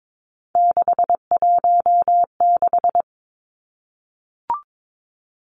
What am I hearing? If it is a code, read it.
Morse code, heard as a keyed tone, about 22 wpm, 700 Hz. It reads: 616
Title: Three Challenging Letter-Number Combinations 22wpm